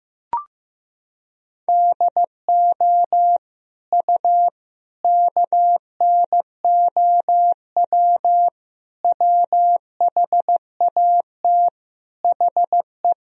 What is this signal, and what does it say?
Morse code, heard as a keyed tone, about 16 wpm, 700 Hz. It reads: DO U KNOW WHAT HE
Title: Sentences from Top 300 Words and Abbreviations 15wpm